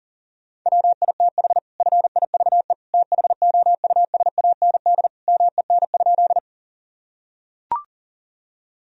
Morse code, heard as a keyed tone, about 37 wpm, 700 Hz. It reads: WITH FIVE THOUSAND MEN?
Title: Sentences from Top 1000 Words - Review Part 3 - 40wpm